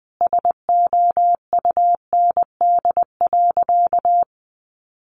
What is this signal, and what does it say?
Morse code, heard as a keyed tone, about 20 wpm, 700 Hz. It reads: SOUND.
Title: Sentences from Top 900 Words 20wpm